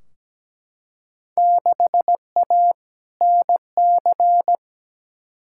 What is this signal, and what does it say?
Morse code, heard as a keyed tone, about 17 wpm, 700 Hz. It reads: 6A NC
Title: Field Day Exchanges Encore - 17wpm